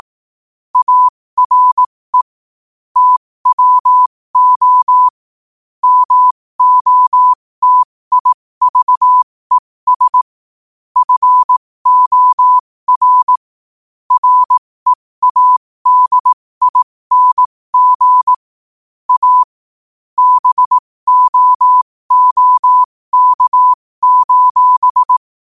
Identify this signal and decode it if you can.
Morse code, heard as a keyed tone, about 18 wpm, spaced as for 15 wpm, 1000 Hz. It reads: ARE TWO MOTIVES FOR READING A BOOK:
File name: qotdfiles / 15wpm